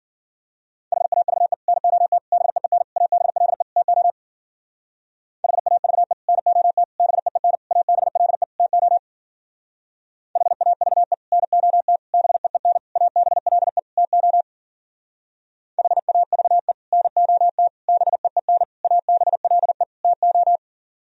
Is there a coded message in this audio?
Yes: Morse code from 0.9 to 20.6 s, heard as a keyed tone, about 51 wpm, 700 Hz. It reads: HAVE NOT BEEN ABLE TO HAVE NOT BEEN ABLE TO HAVE NOT BEEN ABLE TO HAVE NOT BEEN ABLE TO